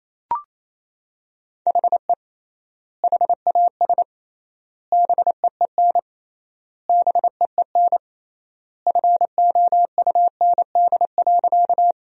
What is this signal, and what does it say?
Morse code, heard as a keyed tone, about 28 wpm, 700 Hz. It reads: HE HAS BEEN BEEN FOUND.
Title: 3x Word-Spacing - Sentences from Top 300 Words - 28wpm